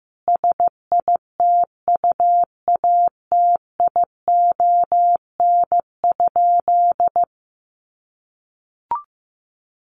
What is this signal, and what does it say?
Morse code, heard as a keyed tone, about 14 wpm, 700 Hz. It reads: SITUATION?
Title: Sentences from Top 1200 Words 15wpm